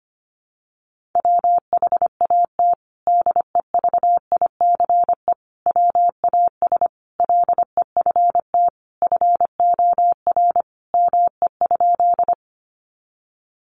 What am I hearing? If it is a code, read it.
Morse code, heard as a keyed tone, about 25 wpm, 700 Hz. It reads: WHAT DEVICE WAS LEFT FOR ME?
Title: Sentences from Top 1500 Words 25wpm